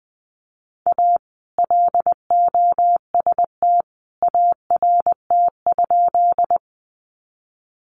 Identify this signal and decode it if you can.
Morse code, heard as a keyed tone, about 20 wpm, 700 Hz. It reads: A LOST ART?
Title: Sentences from Top 1400 Words 20wpm